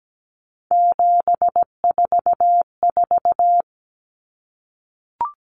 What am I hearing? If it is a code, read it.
Morse code, heard as a keyed tone, about 17 wpm, 700 Hz. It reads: 744
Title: Common Cut Numbers - 3-Digit 17wpm